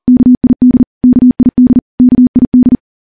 Morse code, heard as a keyed tone, about 40 wpm, 250 Hz. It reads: KID KID KID
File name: morse-training-40WPM